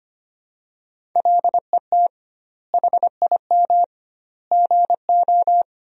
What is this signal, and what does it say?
Morse code, heard as a keyed tone, about 25 wpm, 700 Hz. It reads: LET HIM GO
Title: 2x Word-Spacing - Sentences from Top 500 Words - 25wpm